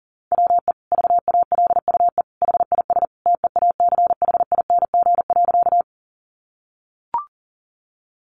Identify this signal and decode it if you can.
Morse code, heard as a keyed tone, about 37 wpm, 700 Hz. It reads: WE VALUE HIS TEACHING.
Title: Sentences from Top 1100 Words 40wpm